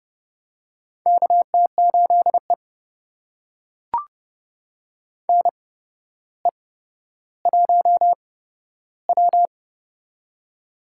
Morse code, heard as a keyed tone, about 29 wpm, spaced as for 13 wpm, 700 Hz. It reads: KT8E NE1W
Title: USA Call Signs - 4 Characters - 8x Character Spacing - 30wpm